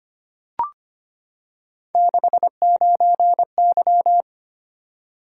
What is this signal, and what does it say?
Morse code, heard as a keyed tone, about 25 wpm, 700 Hz. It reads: 69Y
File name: Three Letter-Number Combinations - 25wpm